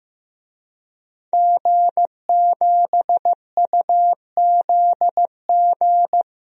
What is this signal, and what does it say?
Morse code, heard as a keyed tone, about 15 wpm, 700 Hz. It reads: G7UZG
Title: DXCC-100 Call Signs - 5 Characters - 15wpm